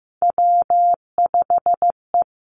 Morse code, heard as a keyed tone, about 15 wpm, 700 Hz. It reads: W5E
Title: Three Challenging Letter-Number Combinations 15wpm